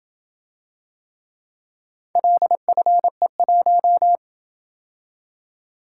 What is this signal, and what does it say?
Morse code, heard as a keyed tone, about 27 wpm, 700 Hz. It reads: LFE1